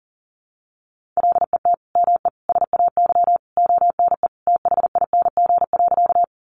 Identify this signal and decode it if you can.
Morse code, heard as a keyed tone, about 37 wpm, 700 Hz. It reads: LET ME SAY ONE THING.
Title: Sentences from Top 300 Words 40wpm